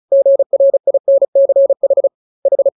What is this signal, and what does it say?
Morse code, heard as a keyed tone, about 34 wpm, 550 Hz. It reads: GRINCH H
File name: jokes / joke 7 35 0000